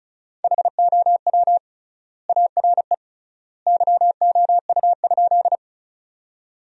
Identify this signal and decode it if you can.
Morse code, heard as a keyed tone, about 36 wpm, 700 Hz. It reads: HOW ARE YOU?
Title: 3x Word-Spacing - Sentences from Top 100 Words - 35wpm